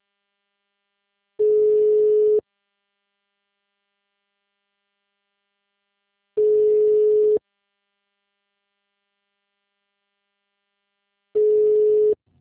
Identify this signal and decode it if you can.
Ringback tone (Europe, 425 Hz).